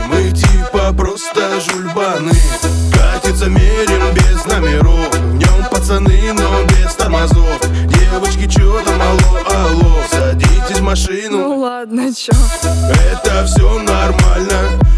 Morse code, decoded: none